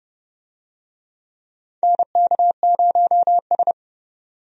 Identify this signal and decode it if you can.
Morse code, heard as a keyed tone, about 30 wpm, 700 Hz. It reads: NK0S
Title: USA Call Signs - 4 Characters - 30wpm